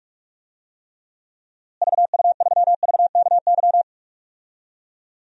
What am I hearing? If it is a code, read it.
Morse code, heard as a keyed tone, about 46 wpm, 700 Hz. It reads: VU3VKY
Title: DXCC-100 Call Signs - 6 Characters - Repeated 3x - 45wpm